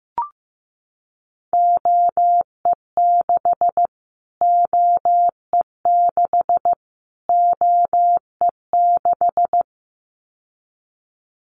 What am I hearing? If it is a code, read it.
Morse code, heard as a keyed tone, about 15 wpm, 700 Hz. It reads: OE6 OE6 OE6
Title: Callsign Prefixes for Top 25 DXCC - Repeated 3x - 15wpm